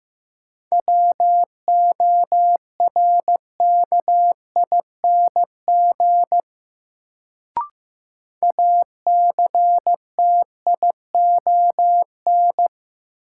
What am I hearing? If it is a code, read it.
Morse code, heard as a keyed tone, about 15 wpm, 700 Hz. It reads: WORKING ACTION
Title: Sentences from Top 700 Words - New Word Drill - 15wpm